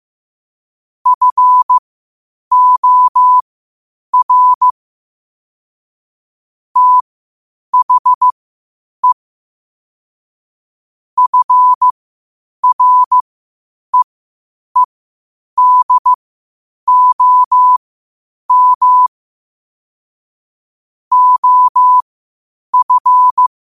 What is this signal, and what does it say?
Morse code, heard as a keyed tone, about 15 wpm, spaced as for 8 wpm, 1000 Hz. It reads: FOR THE FREEDOM OF
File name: qotdfiles / 8wpm